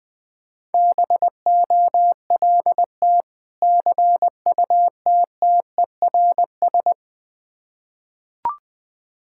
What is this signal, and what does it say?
Morse code, heard as a keyed tone, about 20 wpm, 700 Hz. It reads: BOLT CUTTERS
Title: Difficult Words with Hints - 20wpm